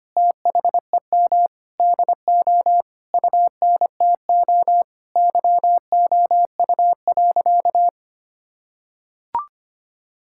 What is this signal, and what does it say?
Morse code, heard as a keyed tone, about 25 wpm, 700 Hz. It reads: THEM DO UNTO YOU.
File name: practice / 25wpm / Sentences with 9-Word Phrases - 25wpm